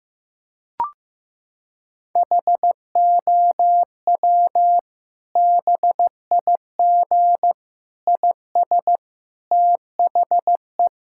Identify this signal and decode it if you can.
Morse code, heard as a keyed tone, about 15 wpm, 700 Hz. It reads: HOW BIG IS THE
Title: Sentences from Top 1000 Words - Review Part 5 - 15wpm